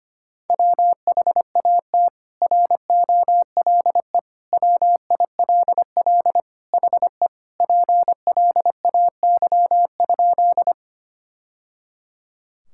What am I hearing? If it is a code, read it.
Morse code, heard as a keyed tone, about 25 wpm, 700 Hz. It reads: WHAT ROLE WILL HE PLAY?
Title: Sentences from Top 1000 Words - Review Part 6 - 25wpm